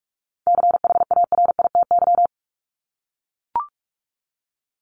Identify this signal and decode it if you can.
Morse code, heard as a keyed tone, about 44 wpm, 700 Hz. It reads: CHARITY
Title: Top 1501-1750 Words 45wpm